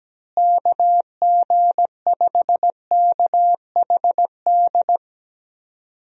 Morse code, heard as a keyed tone, about 17 wpm, 700 Hz. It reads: KG5KHD
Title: USA Call Signs - 6 Characters - Repeated 3x - 17wpm